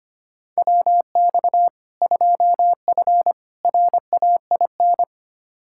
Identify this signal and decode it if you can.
Morse code, heard as a keyed tone, about 25 wpm, 700 Hz. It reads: WX 2F RAIN
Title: K1USN SST QSO Practice 25wpm